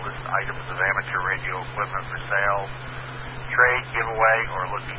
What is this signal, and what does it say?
Emergency Alert System (SAME) tones, heard as harsh buzzing bursts, with speech. It is heard from 0.7 to 3.8 s.